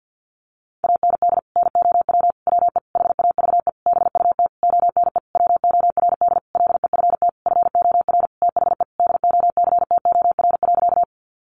Morse code, heard as a keyed tone, about 50 wpm, 700 Hz. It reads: AND NOW WE HAVE BUT ONE WORD LEFT FOR THE DOCTOR.
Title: Sentences from Top 1500 Words 50wpm